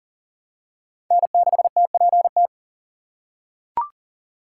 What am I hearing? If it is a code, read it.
Morse code, heard as a keyed tone, about 40 wpm, 690 Hz. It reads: N6TPT
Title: USA Call Signs - 5 Characters - 6x Character Spacing - 40wpm